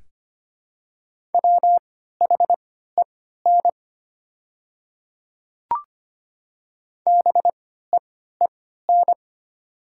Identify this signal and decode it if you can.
Morse code, heard as a keyed tone, about 25 wpm, spaced as for 14 wpm, 700 Hz. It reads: WHEN BEEN
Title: 3x Character Spacing - Top 100 Words - 25wpm